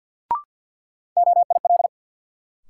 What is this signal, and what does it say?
Morse code, heard as a keyed tone, about 52 wpm, 700 Hz. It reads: OIL
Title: Sentences from Top 1000 Words - Review Part 4 - New Word Drill - 50wpm